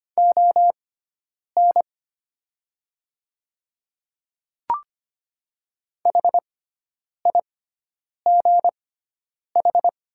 Morse code, heard as a keyed tone, about 25 wpm, spaced as for 9 wpm, 700 Hz. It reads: ON HIGH
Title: 6x Character Spacing - Top 300 Words - 25wpm